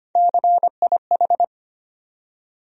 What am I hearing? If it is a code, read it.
Morse code, heard as a keyed tone, about 25 wpm, 700 Hz. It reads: CIH
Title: USA Call Sign Postfixes - 3 Letters 25wpm